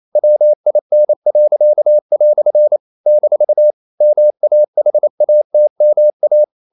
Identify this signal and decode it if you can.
Morse code, heard as a keyed tone, about 28 wpm, 600 Hz. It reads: WIN." - MAHATMA
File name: CW / Practice / EnglishQuotations0228